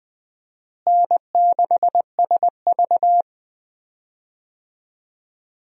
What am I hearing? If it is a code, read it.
Morse code, heard as a keyed tone, about 20 wpm, 700 Hz. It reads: N6SV
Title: USA Call Signs - 4 Characters - Repeated 3x - 20wpm